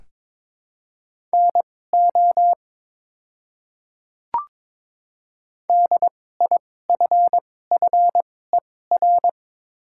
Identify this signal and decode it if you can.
Morse code, heard as a keyed tone, about 22 wpm, spaced as for 16 wpm, 700 Hz. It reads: NO DIFFER
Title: 2x Character Spacing - Top 200 Words - 22wpm